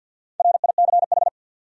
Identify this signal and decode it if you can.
Morse code, heard as a keyed tone, about 53 wpm, 700 Hz. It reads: RICH
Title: USA Common Names - Max 4 Letters - 50wpm